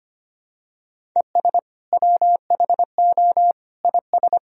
Morse code, heard as a keyed tone, about 25 wpm, 700 Hz. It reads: ES WHO IS
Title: Sentences from Top 100 Words and Abbreviations 25wpm